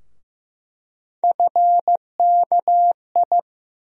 Morse code, heard as a keyed tone, about 15 wpm, 700 Hz. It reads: FKI
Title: USA Call Sign Postfixes - 3 Letters 15wpm